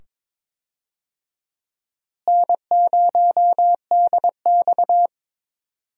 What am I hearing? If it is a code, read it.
Morse code, heard as a keyed tone, about 22 wpm, 700 Hz. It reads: N0DX
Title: USA Call Signs - 4 Characters - Repeated 2x - 22wpm